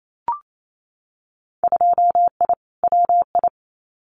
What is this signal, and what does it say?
Morse code, heard as a keyed tone, about 28 wpm, 700 Hz. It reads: 2I WI
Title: Winter Field Day Exchanges 28wpm